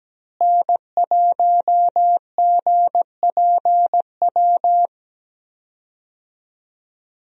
Morse code, heard as a keyed tone, about 17 wpm, 700 Hz. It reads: N1GPW
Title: USA Call Signs - 5 Characters - Repeated 2x - 17wpm